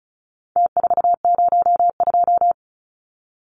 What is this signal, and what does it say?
Morse code, heard as a keyed tone, about 34 wpm, 700 Hz. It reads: T402
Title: Numbers - Contest Style - 4-Digit 35wpm